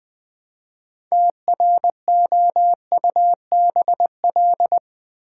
Morse code, heard as a keyed tone, about 20 wpm, 700 Hz. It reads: TROUBL
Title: Build A Word - 3 Letters At A Time - 8-Letter Words - 20wpm